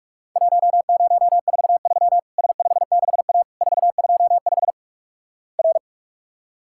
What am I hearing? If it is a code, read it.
Morse code, heard as a keyed tone, about 48 wpm, 700 Hz. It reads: 1043 S56A 425 R
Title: Worked All Europe DX Contest - QTC Exchange - 45wpm